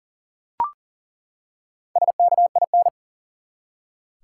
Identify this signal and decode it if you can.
Morse code, heard as a keyed tone, about 37 wpm, 700 Hz. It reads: SKIN
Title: Four Letter Words - 40wpm